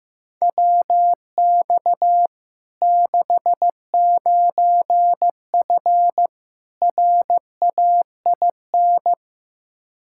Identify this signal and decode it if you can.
Morse code, heard as a keyed tone, about 15 wpm, 700 Hz. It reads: WX 69F RAIN
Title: Practice QSO - Exchange 2 - Info - 15wpm